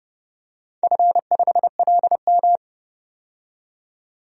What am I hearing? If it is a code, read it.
Morse code, heard as a keyed tone, about 32 wpm, 700 Hz. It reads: F5LM
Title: DXCC-100 Call Signs - 4 Characters - 30wpm